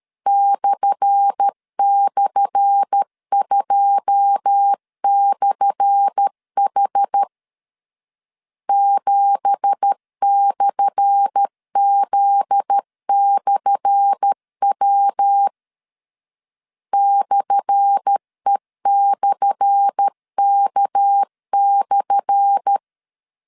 Morse code, heard as a keyed tone, about 13 wpm, 800 Hz. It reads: //2/H 7/Z/W /E/K/
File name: LEZIONI / 10 SIGNS / 10A SLASH 12wpm 2314